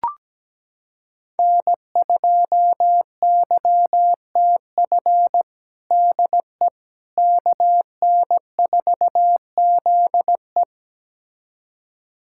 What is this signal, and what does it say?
Morse code, heard as a keyed tone, about 17 wpm, 700 Hz. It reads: N2YTF DE KN4ZE
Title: USA Callsign DE Callsign - 5 Characters - 17wpm